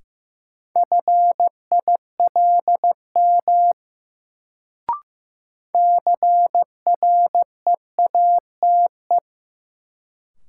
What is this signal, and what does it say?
Morse code, heard as a keyed tone, about 15 wpm, 700 Hz. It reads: FILM CREATE
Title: Sentences from Top 1000 Words - Review Part 3 - New Word Drill - 15wpm